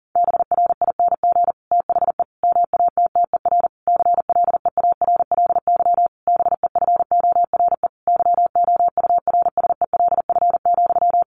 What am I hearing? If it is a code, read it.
Morse code, heard as a keyed tone, about 40 wpm, 700 Hz. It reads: BRING THE MATTER CLEARLY BEFORE YOURSELF,
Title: Sentences from Top 1300 Words 40wpm